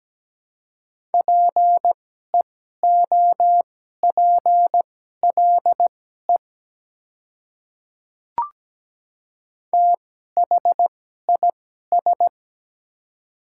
Morse code, heard as a keyed tone, about 17 wpm, spaced as for 12 wpm, 700 Hz. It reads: PEOPLE THIS
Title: 2x Character Spacing - Top 100 Words - 17wpm